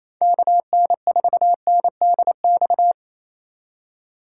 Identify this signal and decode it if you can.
Morse code, heard as a keyed tone, about 28 wpm, 700 Hz. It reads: KN4NDX
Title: USA Call Signs - 6 Characters - 4x Character Spacing - 28wpm